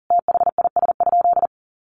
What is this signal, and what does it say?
Morse code, heard as a keyed tone, about 40 wpm, 700 Hz. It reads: THIS?
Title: Sentences from Top 600 Words 40wpm